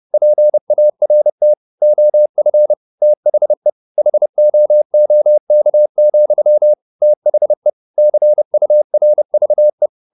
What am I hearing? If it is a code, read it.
Morse code, heard as a keyed tone, about 30 wpm, 600 Hz. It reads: PART OF THE HOOK, THE CURVE
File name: the old man and the sea / the old man and the sea 1 30wpm